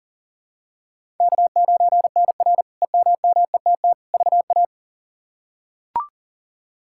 Morse code, heard as a keyed tone, about 40 wpm, 700 Hz. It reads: K9NR EMMETT VA
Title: K1USN SST QSO Practice - 40wpm